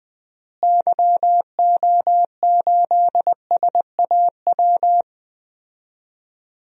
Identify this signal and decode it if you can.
Morse code, heard as a keyed tone, about 21 wpm, 700 Hz. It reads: YO8SAW
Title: DXCC-100 Call Signs - 6 Characters - Repeated 3x - 20wpm